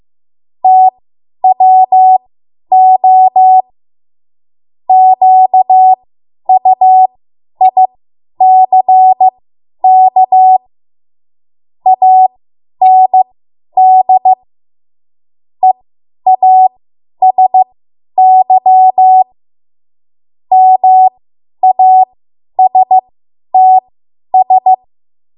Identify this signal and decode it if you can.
Morse code, heard as a keyed tone, about 15 wpm, spaced as for 10 wpm, 750 Hz. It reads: TWO QUICK AND EASY MASTS
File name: Morse / 10WPM